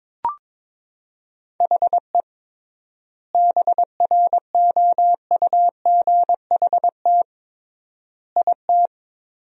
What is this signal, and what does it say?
Morse code, heard as a keyed tone, about 22 wpm, 700 Hz. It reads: HE BROUGHT IT
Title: 3x Word-Spacing - Sentences from Top 500 Words - 22wpm